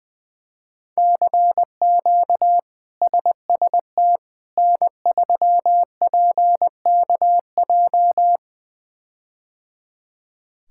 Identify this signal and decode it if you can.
Morse code, heard as a keyed tone, about 20 wpm, 700 Hz. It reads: CQ SST N3PKJ
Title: K1USN SST QSO Practice - 20wpm